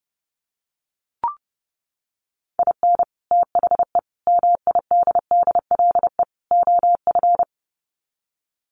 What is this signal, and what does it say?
Morse code, heard as a keyed tone, about 30 wpm, 700 Hz. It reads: IN THE MIDDLE OF
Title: Sets of 4 Words 30wpm